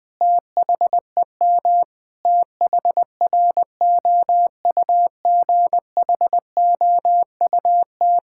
Morse code, heard as a keyed tone, about 20 wpm, 700 Hz. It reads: THEM THROUGHOUT
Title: Sentences from Top 1000 Words - Review Part 5 - 20wpm